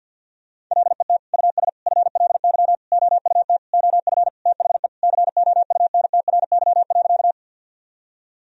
Morse code, heard as a keyed tone, about 51 wpm, 700 Hz. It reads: LET US FLY OUT OF THE COUNTRY.
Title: Sentences from Top 1000 Words - Review Part 2 - 50wpm